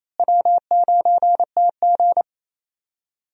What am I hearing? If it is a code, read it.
Morse code, heard as a keyed tone, about 28 wpm, 700 Hz. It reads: W9TG
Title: USA Call Signs - 4 Characters - Repeated 2x - 28wpm